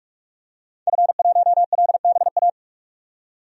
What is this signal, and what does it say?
Morse code, heard as a keyed tone, about 46 wpm, 700 Hz. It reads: F1LBA
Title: DXCC-100 Call Signs - 5 Characters - 45wpm